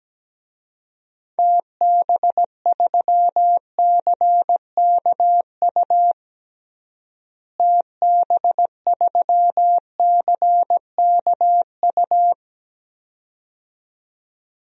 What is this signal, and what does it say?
Morse code, heard as a keyed tone, about 17 wpm, 700 Hz. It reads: TB3CKU TB3CKU
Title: DXCC-100 Call Signs - 6 Characters - Repeated 2x - 17wpm